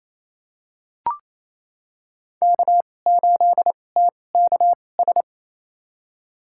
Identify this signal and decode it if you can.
Morse code, heard as a keyed tone, about 28 wpm, spaced as for 20 wpm, 700 Hz. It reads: K8TKS
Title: USA Call Signs - 5 Characters - 2x Character Spacing - 28wpm